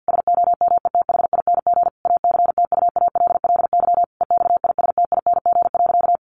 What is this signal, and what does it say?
Morse code, heard as a keyed tone, about 50 wpm, 700 Hz. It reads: SOMETHING ACTUALLY EXISTING.
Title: Sentences from Top 1500 Words 50wpm